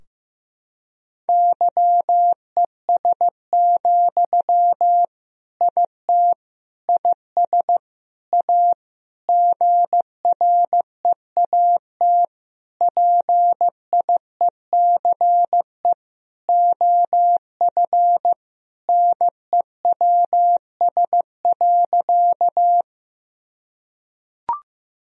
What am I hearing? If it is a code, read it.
Morse code, heard as a keyed tone, about 15 wpm, 700 Hz. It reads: YES, IT IS A GREAT PIECE OF NEWS.
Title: Sentences from Top 1000 Words - Review Part 2 - 15wpm